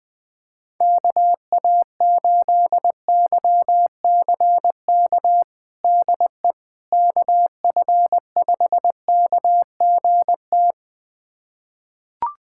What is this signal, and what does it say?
Morse code, heard as a keyed tone, about 20 wpm, 700 Hz. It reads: KA8YCK DE KF5KGT
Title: USA Callsign DE Callsign - 6 Characters - 20wpm